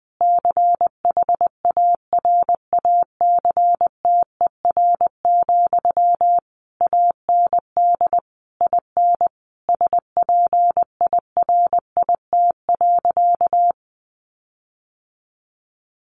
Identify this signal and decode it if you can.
Morse code, heard as a keyed tone, about 20 wpm, 700 Hz. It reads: CHARACTER, AND IN SPIRIT.
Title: Sentences from Top 1400 Words 20wpm